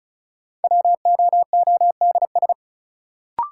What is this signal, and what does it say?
Morse code, heard as a keyed tone, about 36 wpm, 700 Hz. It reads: WOODS
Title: Sentences from Top 1000 Words - Review Part 2 - New Word Drill - 35wpm